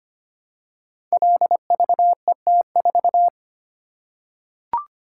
Morse code, heard as a keyed tone, about 25 wpm, 700 Hz. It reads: LVET4
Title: Code Groups - 5 Letter:Numbers - 25wpm